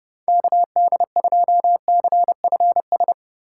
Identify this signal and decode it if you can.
Morse code, heard as a keyed tone, about 37 wpm, 700 Hz. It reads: KD2CFS
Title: Speed Racing - USA 2x3 Call Signs 30wpm